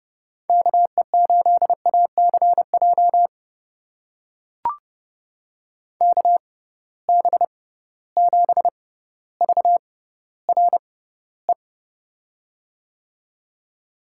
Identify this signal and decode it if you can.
Morse code, heard as a keyed tone, about 32 wpm, spaced as for 16 wpm, 700 Hz. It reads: KE8ACJ KB7VRE
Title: USA Call Signs - 6 Characters - 6x Character Spacing - 30wpm